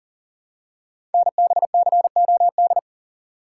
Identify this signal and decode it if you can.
Morse code, heard as a keyed tone, about 40 wpm, 700 Hz. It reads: NBCOD
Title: Code Groups - 5 Letters - 40wpm